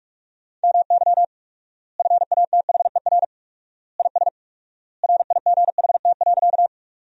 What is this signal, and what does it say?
Morse code, heard as a keyed tone, about 46 wpm, 700 Hz. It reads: MY FATHER IS RIGHT.
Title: 4x Word-Spacing - Sentences from Top 300 Words - 45wpm